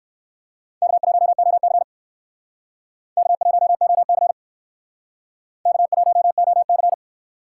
Morse code, heard as a keyed tone, about 59 wpm, 700 Hz. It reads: K1OC K1OC K1OC